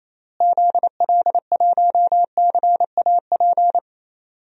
Morse code, heard as a keyed tone, about 28 wpm, 700 Hz. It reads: ZL1CAP
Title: DXCC-100 Call Signs - 6 Characters - Repeated 2x - 28wpm